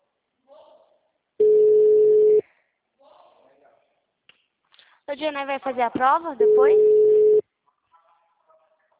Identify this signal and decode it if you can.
Ringback tone (Europe, 425 Hz), with speech.
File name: out- -2017- - - .5862